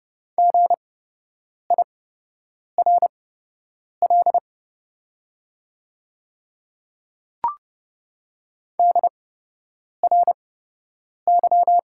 Morse code, heard as a keyed tone, about 30 wpm, spaced as for 8 wpm, 700 Hz. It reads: GIRL DRY
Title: 8x Character Spacing - Top 500 Words - 30wpm